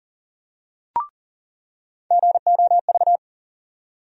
Morse code, heard as a keyed tone, about 40 wpm, 700 Hz. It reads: GOV
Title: USA Call Sign Postfixes - 3 Letters 40wpm